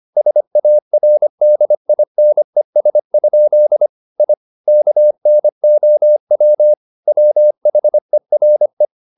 Morse code, heard as a keyed tone, about 25 wpm, 600 Hz. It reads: SARDINES? I KNOW WHERE
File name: the old man and the sea part 1 25wpm